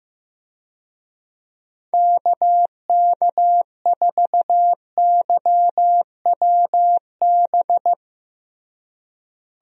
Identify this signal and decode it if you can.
Morse code, heard as a keyed tone, about 15 wpm, 700 Hz. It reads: KK4YWB
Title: USA Call Signs - 6 Characters - Repeated 3x - 15wpm